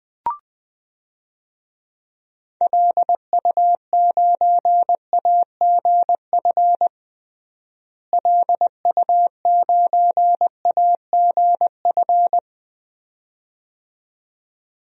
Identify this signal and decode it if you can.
Morse code, heard as a keyed tone, about 21 wpm, 700 Hz. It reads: LU9AGF LU9AGF
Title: DXCC-100 Call Signs - 6 Characters - Repeated 2x - 20wpm